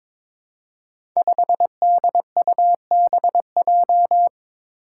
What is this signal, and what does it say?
Morse code, heard as a keyed tone, about 22 wpm, 700 Hz. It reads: 5DUBJ
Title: Five Challenging Letter-Number Combinations 22wpm